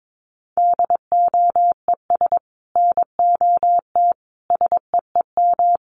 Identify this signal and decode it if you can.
Morse code, heard as a keyed tone, about 22 wpm, 700 Hz. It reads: DOES NOT SEEM
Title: Sets of 5 Words 22wpm